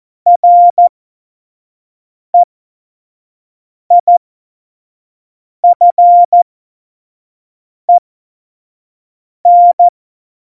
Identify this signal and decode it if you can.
Morse code, heard as a keyed tone, about 14 wpm, spaced as for 5 wpm, 700 Hz. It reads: REIFEN
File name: Morsekurs / Witze / Witz 12. Februar 2019, 25 F70